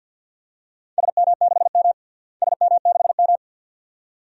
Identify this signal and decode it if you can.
Morse code, heard as a keyed tone, about 51 wpm, 700 Hz. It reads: SM6M SM6M
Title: DXCC-100 Call Signs - 4 Characters - Repeated 2x - 50wpm